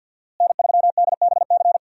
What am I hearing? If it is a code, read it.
Morse code, heard as a keyed tone, about 52 wpm, 700 Hz. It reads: N3DBC
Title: USA Call Signs - 5 Characters - Repeated 2x - 50wpm